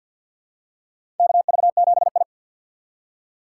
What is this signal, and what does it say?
Morse code, heard as a keyed tone, about 47 wpm, 700 Hz. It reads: KV7I
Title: USA Call Signs - 4 Characters - 6x Character Spacing - 50wpm